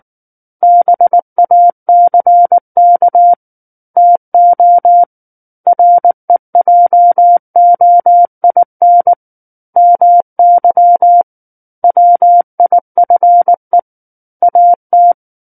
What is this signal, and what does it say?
Morse code, heard as a keyed tone, about 19 wpm, 700 Hz. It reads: BACK TO REJOIN MY WIFE AT
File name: Book One / 10-Chapter 10 - 19 WPM